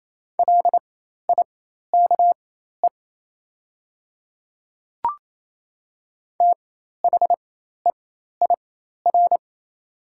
Morse code, heard as a keyed tone, about 28 wpm, spaced as for 13 wpm, 700 Hz. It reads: LIKE THEIR
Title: 4x Character Spacing - Top 300 Words - 28wpm